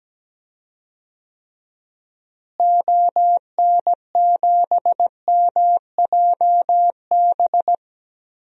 Morse code, heard as a keyed tone, about 17 wpm, 700 Hz. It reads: ON7MJB
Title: DXCC-100 Call Signs - 6 Characters - Repeated 3x - 17wpm